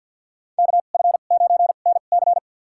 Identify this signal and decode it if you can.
Morse code, heard as a keyed tone, about 47 wpm, spaced as for 35 wpm, 700 Hz. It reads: KF9NC
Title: USA Call Signs - 5 Characters - 2x Character Spacing - 50wpm